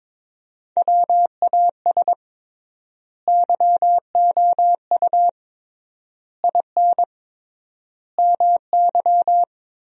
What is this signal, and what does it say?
Morse code, heard as a keyed tone, about 22 wpm, 700 Hz. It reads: WAS YOU IN MY
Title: 3x Word-Spacing - Sentences from Top 300 Words - 22wpm